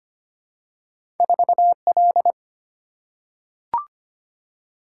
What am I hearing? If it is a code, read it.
Morse code, heard as a keyed tone, about 25 wpm, 700 Hz. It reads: 4L
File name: Two Letter-Number Combinations - 25wpm